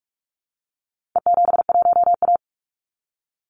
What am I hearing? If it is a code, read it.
Morse code, heard as a keyed tone, about 44 wpm, 700 Hz. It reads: E71A
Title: Call Signs - 45wpm